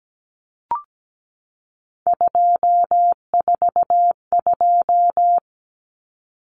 Morse code, heard as a keyed tone, about 17 wpm, 700 Hz. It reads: 242
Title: Numbers - 3-Digit 17wpm